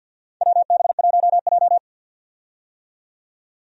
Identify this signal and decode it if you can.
Morse code, heard as a keyed tone, about 50 wpm, 700 Hz. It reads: WB1J
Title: USA Call Signs - 4 Characters - 50wpm